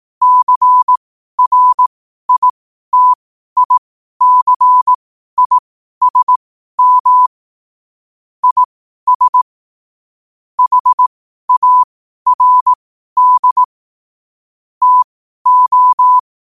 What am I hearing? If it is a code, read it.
Morse code, heard as a keyed tone, about 18 wpm, spaced as for 12 wpm, 1000 Hz. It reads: CRITICISM IS HARD TO
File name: qotdfiles / 12wpm